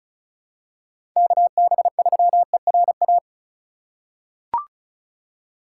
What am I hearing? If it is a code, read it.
Morse code, heard as a keyed tone, about 36 wpm, 700 Hz. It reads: KB3ERA
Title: USA Call Signs - 6 Characters - 4x Character Spacing - 35wpm